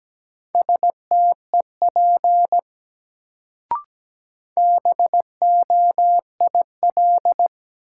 Morse code, heard as a keyed tone, about 17 wpm, 700 Hz. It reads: STEP BOIL
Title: Sets of 5 Words Encore - New Word Drill - 17wpm